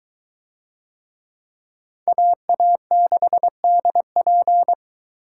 Morse code, heard as a keyed tone, about 21 wpm, 700 Hz. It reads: AA6DP